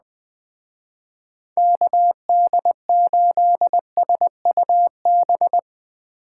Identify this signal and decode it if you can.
Morse code, heard as a keyed tone, about 20 wpm, 700 Hz. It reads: KD8SUB
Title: Speed Racing - USA Call Signs 17wpm